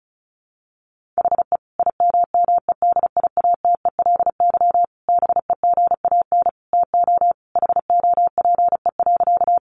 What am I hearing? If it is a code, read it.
Morse code, heard as a keyed tone, about 34 wpm, 700 Hz. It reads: HE IMMEDIATELY BEGAN TO HOPE.
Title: Sentences from Top 1100 Words 35wpm